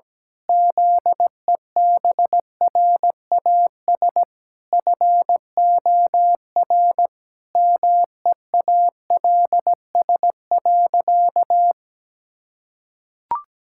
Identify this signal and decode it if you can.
Morse code, heard as a keyed tone, about 17 wpm, 700 Hz. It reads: ZEBRAS FOR MEALS.
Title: Pangram Challenge v2 - 17wpm